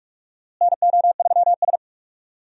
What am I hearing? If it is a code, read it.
Morse code, heard as a keyed tone, about 42 wpm, 700 Hz. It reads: NO3S
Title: USA Call Signs - 4 Characters - Repeated 3x - 45wpm